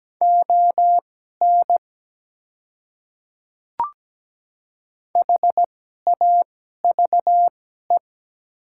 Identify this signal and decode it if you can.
Morse code, heard as a keyed tone, about 17 wpm, spaced as for 12 wpm, 700 Hz. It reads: ON HAVE
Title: 2x Character Spacing - Top 100 Words - 17wpm